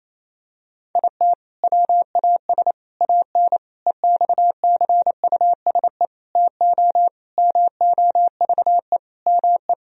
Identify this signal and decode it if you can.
Morse code, heard as a keyed tone, about 28 wpm, 700 Hz. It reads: IT WAS AN EXCUSE TO MOVE ME
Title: Sentences from Top 1000 Words - Review Part 1 - 28wpm